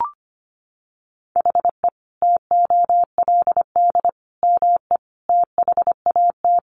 Morse code, heard as a keyed tone, about 25 wpm, 700 Hz. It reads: HE TOLD ME THAT
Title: Sentences from Top 1300 Words 25wpm